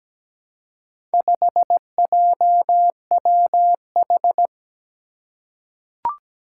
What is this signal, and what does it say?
Morse code, heard as a keyed tone, about 17 wpm, 700 Hz. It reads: 5JWH
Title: Four Challenging Letter-Number Combinations 17wpm